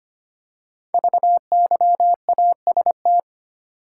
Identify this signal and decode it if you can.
Morse code, heard as a keyed tone, about 25 wpm, 700 Hz. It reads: VYAST
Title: Code Groups - 5 Letters - 25wpm